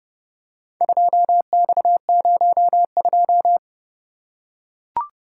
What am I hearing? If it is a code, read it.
Morse code, heard as a keyed tone, about 29 wpm, 700 Hz. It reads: 2X02